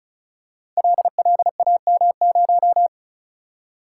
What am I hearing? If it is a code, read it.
Morse code, heard as a keyed tone, about 36 wpm, 700 Hz. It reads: LLAM0
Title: Code Groups - 5 Letter:Numbers - 35wpm